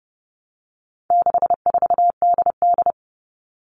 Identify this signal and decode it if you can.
Morse code, heard as a keyed tone, about 30 wpm, 700 Hz. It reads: 64DD
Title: All Cut Numbers - 4-Digit 30wpm